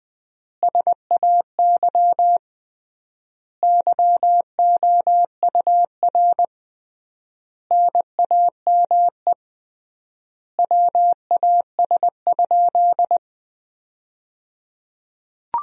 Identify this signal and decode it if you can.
Morse code, heard as a keyed tone, about 21 wpm, 700 Hz. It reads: SAY YOUR NAME WAS?
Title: 3x Word-Spacing - Sentences from Top 200 Words - 20wpm